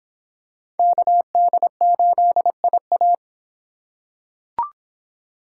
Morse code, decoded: KD8IA

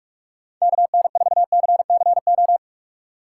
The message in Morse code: KN4CCO

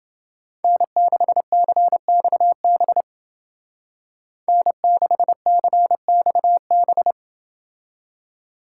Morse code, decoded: N6CXB N6CXB

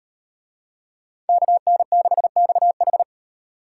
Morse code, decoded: KN6XH

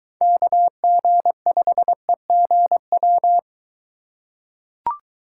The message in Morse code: KG5EGW

